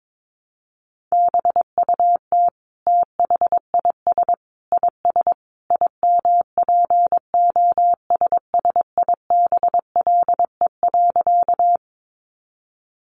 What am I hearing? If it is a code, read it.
Morse code, heard as a keyed tone, about 22 wpm, 700 Hz. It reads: BUT THIS IS IMPOSSIBLE.
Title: Sentences from Top 1100 Words 22wpm